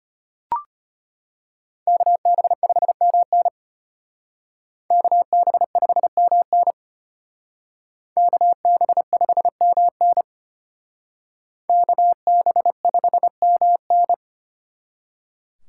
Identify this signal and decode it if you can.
Morse code, heard as a keyed tone, about 33 wpm, 700 Hz. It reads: KB5MN KB5MN KB5MN KB5MN